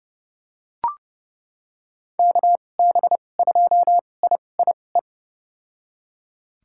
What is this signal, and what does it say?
Morse code, heard as a keyed tone, about 30 wpm, spaced as for 22 wpm, 700 Hz. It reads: KB2IIE
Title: USA Call Signs - 6 Characters - 2x Character Spacing - 30wpm